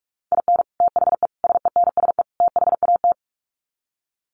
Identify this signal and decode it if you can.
Morse code, heard as a keyed tone, about 44 wpm, 700 Hz. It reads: IN THE SENSE THAT
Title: Sets of 4 Words 45wpm